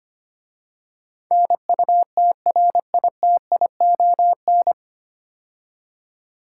Morse code, heard as a keyed tone, about 25 wpm, 700 Hz. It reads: NUTRITION
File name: practice / 25wpm / 9 Letter Words - First 3 Letters Given - 25wpm